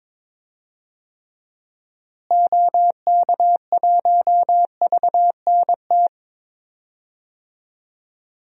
Morse code, heard as a keyed tone, about 22 wpm, 700 Hz. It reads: OK1VNT
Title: DXCC-100 Call Signs - 6 Characters - Repeated 2x - 22wpm